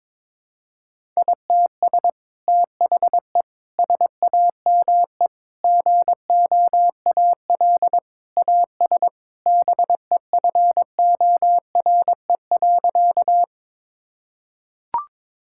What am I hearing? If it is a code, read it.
Morse code, heard as a keyed tone, about 22 wpm, 700 Hz. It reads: ITS THE SAME GOAL AS BEFORE.
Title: Sentences from Top 1000 Words - Review Part 5 - 22wpm